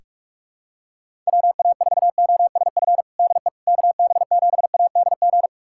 Morse code, heard as a keyed tone, about 46 wpm, 700 Hz. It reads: WA4OSF DE KB7ADG
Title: USA Callsign DE Callsign - 6 Characters - 45wpm